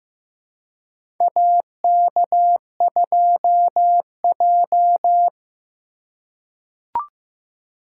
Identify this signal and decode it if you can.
Morse code, heard as a keyed tone, about 15 wpm, 700 Hz. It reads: AK2J
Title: USA Call Signs - 2x Character Spacing - 15wpm